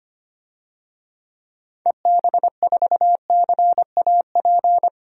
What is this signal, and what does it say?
Morse code, heard as a keyed tone, about 25 wpm, 700 Hz. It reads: EB4CAP
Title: DXCC-100 Call Signs - 6 Characters - 25wpm